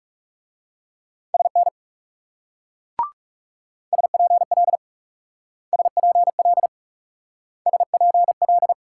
Morse code, heard as a keyed tone, about 38 wpm, 700 Hz. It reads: SN SPL SPL SPL
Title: Speed-Racing Multi-letter Phonemes - 30wpm